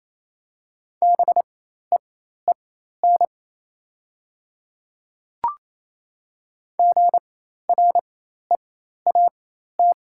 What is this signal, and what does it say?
Morse code, heard as a keyed tone, about 28 wpm, spaced as for 13 wpm, 690 Hz. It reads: BEEN GREAT
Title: 4x Character Spacing - Top 200 Words - 28wpm